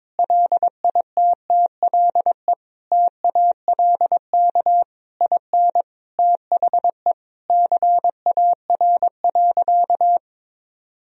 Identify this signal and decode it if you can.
Morse code, heard as a keyed tone, about 22 wpm, 700 Hz. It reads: LITTLE TALK IN THE CAR.
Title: Sentences from Top 1000 Words - Review Part 1 - 22wpm